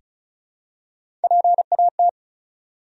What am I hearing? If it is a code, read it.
Morse code, heard as a keyed tone, about 34 wpm, 700 Hz. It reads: PAT
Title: USA Common Names - 35wpm